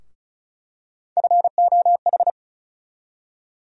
Morse code, heard as a keyed tone, about 36 wpm, 700 Hz. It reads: FOH
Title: USA Call Sign Postfixes - 3 Letters 35wpm